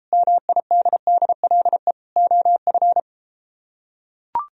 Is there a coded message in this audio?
Yes: Morse code from 0.1 to 3.0 s, heard as a keyed tone, about 32 wpm, 700 Hz. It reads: MIDDLE OF